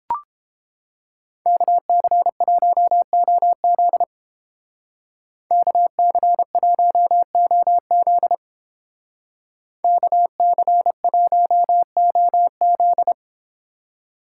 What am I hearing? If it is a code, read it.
Morse code, heard as a keyed tone, about 29 wpm, 700 Hz. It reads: KC1OZ KC1OZ KC1OZ